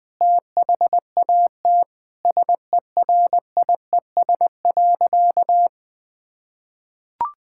Morse code, heard as a keyed tone, about 20 wpm, 700 Hz. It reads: THAT SERIES.
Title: Sentences from Top 1000 Words - Review Part 5 - 20wpm